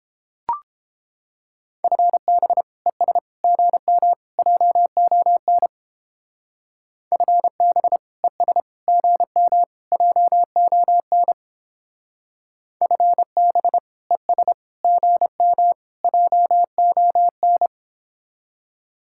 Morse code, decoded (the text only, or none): FB ES GM JON FB ES GM JON FB ES GM JON